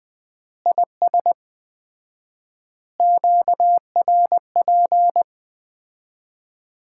Morse code, heard as a keyed tone, about 20 wpm, 700 Hz. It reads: IS QRP
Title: Common QSO Phrases - 4x Word-Spacing - 20wpm